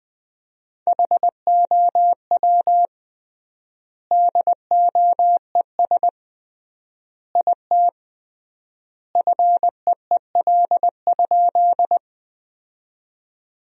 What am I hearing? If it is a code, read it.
Morse code, heard as a keyed tone, about 20 wpm, 700 Hz. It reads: HOW DOES IT FEEL?
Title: 3x Word-Spacing - Sentences from Top 400 Words - 20wpm